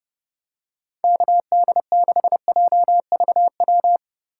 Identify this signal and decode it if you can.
Morse code, heard as a keyed tone, about 32 wpm, 700 Hz. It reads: KD6JVW
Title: USA Call Signs - 6 Characters - 3x Character Spacing - 30wpm